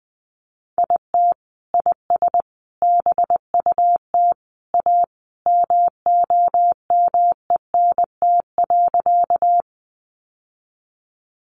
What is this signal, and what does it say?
Morse code, heard as a keyed tone, about 20 wpm, 700 Hz. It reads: IT IS BUT A MOMENT.
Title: Sentences from Top 1000 Words 20wpm